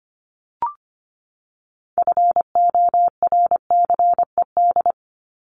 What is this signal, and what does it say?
Morse code, heard as a keyed tone, about 25 wpm, 700 Hz. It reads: FORCED
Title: ED Suffix - Top 100 Words with Base Words 25wpm